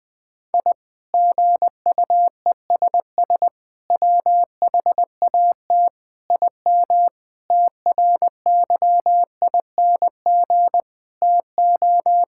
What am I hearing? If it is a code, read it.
Morse code, heard as a keyed tone, about 20 wpm, 700 Hz. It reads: I GUESS WHAT IM TRYING TO
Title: Sentences with 8-Word Phrases - 20wpm